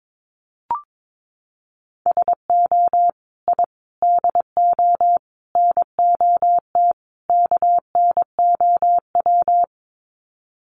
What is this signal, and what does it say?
Morse code, heard as a keyed tone, about 22 wpm, 700 Hz. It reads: SO I DO NOT KNOW
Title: Sets of 5 Words Encore 22wpm